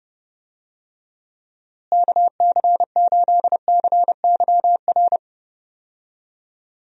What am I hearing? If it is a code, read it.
Morse code, heard as a keyed tone, about 29 wpm, 700 Hz. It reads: KC8CYR
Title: USA Call Signs - 6 Characters - Repeated 3x - 30wpm